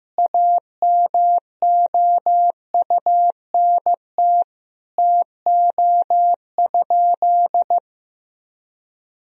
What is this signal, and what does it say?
Morse code, heard as a keyed tone, about 15 wpm, 700 Hz. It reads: AMOUNT TO?
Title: Sentences from Top 1000 Words - Review Part 4 - 15wpm